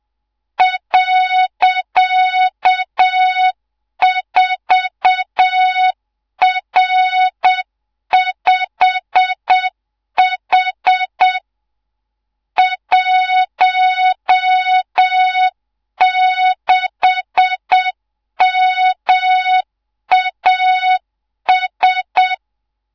Morse code, heard as a keyed tone, about 7 wpm, 3700 Hz, with noise. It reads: .4R5H 16MAS
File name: Morze-HA2MN / hangp / 2 03 GYAKORLAT